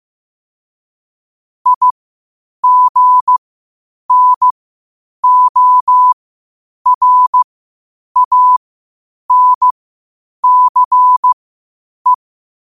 Morse code, heard as a keyed tone, about 15 wpm, spaced as for 8 wpm, 1000 Hz. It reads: IGNORANCE
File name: qotdfiles / 8wpm